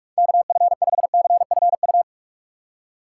Morse code, heard as a keyed tone, about 46 wpm, 700 Hz. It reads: KF5CFU
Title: USA Call Signs - 6 Characters - 2x Character Spacing - 45wpm